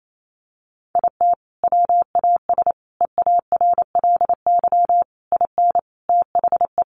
Morse code, heard as a keyed tone, about 28 wpm, 700 Hz. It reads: IT WAS EARLY IN THE
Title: Sentences from Top 1200 Words 28wpm